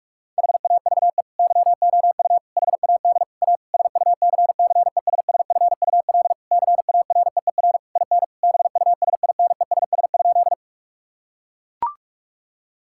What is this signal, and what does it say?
Morse code, heard as a keyed tone, about 41 wpm, 700 Hz. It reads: HAVE YOU HAD A SUCCESSFUL CAREER IN BUSINESS?
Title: Sentences from Top 1000 Words - Review Part 6 - 45wpm